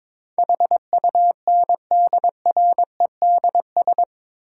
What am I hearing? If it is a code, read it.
Morse code, heard as a keyed tone, about 22 wpm, 700 Hz. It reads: HUNDREDS
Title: Sets of 4 Words - New Word Drill - 22wpm